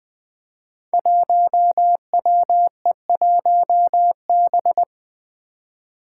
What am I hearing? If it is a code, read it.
Morse code, heard as a keyed tone, about 20 wpm, 700 Hz. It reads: 1WE1B
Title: Five Challenging Letter-Number Combinations 20wpm